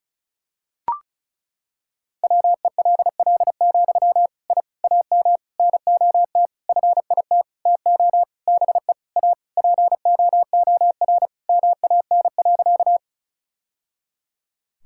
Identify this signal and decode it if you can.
Morse code, heard as a keyed tone, about 36 wpm, 700 Hz. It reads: WELL, I AM NOT FIT TO BE A POOR MAN.
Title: Sentences from Top 1000 Words - Review Part 5 - 35wpm